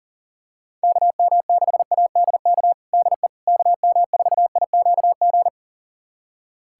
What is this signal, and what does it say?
Morse code, heard as a keyed tone, about 37 wpm, 700 Hz. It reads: KM6ADK DE KM4IQG
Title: USA Callsign DE Callsign - 6 Characters - 40wpm